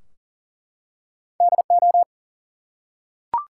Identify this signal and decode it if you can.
Morse code, heard as a keyed tone, about 39 wpm, 700 Hz. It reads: DO
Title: Two Letter Words - 40wpm